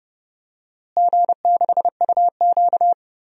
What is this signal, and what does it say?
Morse code, heard as a keyed tone, about 30 wpm, 700 Hz. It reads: G6UQ
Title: DXCC-100 Call Signs - 4 Characters - Repeated 3x - 30wpm